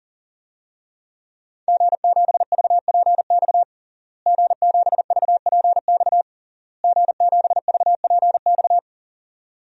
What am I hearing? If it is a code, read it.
Morse code, heard as a keyed tone, about 40 wpm, 700 Hz. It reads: G7VPX G7VPX G7VPX
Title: DXCC-100 Call Signs - 5 Characters - Repeated 3x - 40wpm